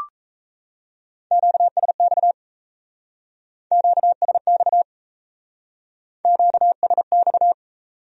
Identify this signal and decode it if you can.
Morse code, heard as a keyed tone, about 34 wpm, 700 Hz. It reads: QSX QSX QSX
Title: Speed-Racing Q-Codes - 28wpm